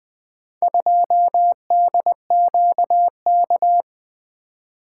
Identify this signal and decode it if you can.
Morse code, heard as a keyed tone, about 21 wpm, 700 Hz. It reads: 2DQK
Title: Four Letter-Number Combinations - 20wpm